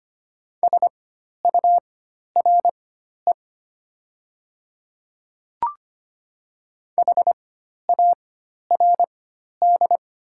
Morse code, heard as a keyed tone, about 25 wpm, spaced as for 12 wpm, 700 Hz. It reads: SURE HARD
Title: 4x Character Spacing - Top 300 Words - 25wpm